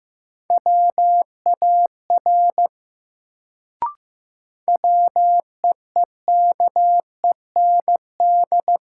Morse code, heard as a keyed tone, about 15 wpm, 700 Hz. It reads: WAR WEEKEND
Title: Sets of 5 Words - New Word Drill - 15wpm